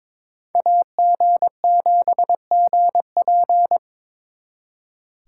Morse code, heard as a keyed tone, about 27 wpm, 700 Hz. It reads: AG7GP
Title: Speed Racing - USA Call Signs 22wpm